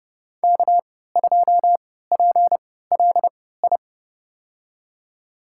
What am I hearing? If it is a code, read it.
Morse code, heard as a keyed tone, about 31 wpm, spaced as for 17 wpm, 700 Hz. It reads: K2PLI
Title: USA Call Signs - 3x Character Spacing - 30wpm